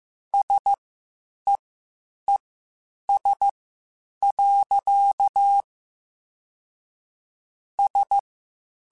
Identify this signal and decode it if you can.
Morse code, heard as a keyed tone, about 15 wpm, spaced as for 8 wpm, 800 Hz. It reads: SEES. S
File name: (spaced) / QSOs Spaced-mp3s / WD0FBT 1 @13 sp